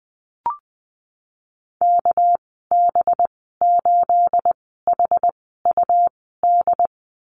Morse code, heard as a keyed tone, about 20 wpm, spaced as for 15 wpm, 700 Hz. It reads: KB8HUD
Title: 238 Lesson 38g - Letter q - Call Signs 20wpm